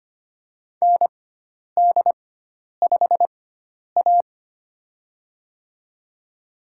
Morse code, heard as a keyed tone, about 25 wpm, spaced as for 10 wpm, 700 Hz. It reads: ND5A
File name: Lesson 16g - Letter c - Call Signs 25wpm